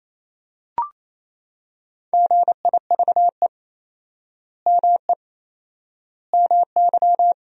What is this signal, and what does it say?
Morse code, heard as a keyed tone, about 28 wpm, 700 Hz. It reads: GIVE ME MY
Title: 4x Word-Spacing - Sentences from Top 200 Words - 28wpm